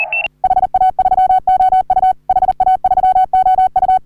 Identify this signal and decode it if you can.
Morse code, heard as a keyed tone, about 40 wpm, 750 Hz. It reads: HA3OU HA3OU
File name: CW / HA3OU RHOBCW REC